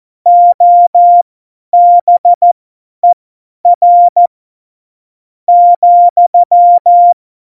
Morse code, heard as a keyed tone, about 14 wpm, spaced as for 8 wpm, 700 Hz. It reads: OBER,
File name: Morsekurs / Witze / Witz 09. April 2019, 50 F70